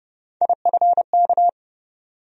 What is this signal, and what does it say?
Morse code, heard as a keyed tone, about 30 wpm, 700 Hz. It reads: IFK
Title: USA Call Sign Postfixes - 3 Letters 30wpm